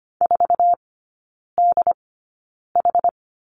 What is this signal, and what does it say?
Morse code, heard as a keyed tone, about 25 wpm, spaced as for 9 wpm, 690 Hz. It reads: 4DH
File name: Lesson 15e - Number 5 - Three Characters 25wpm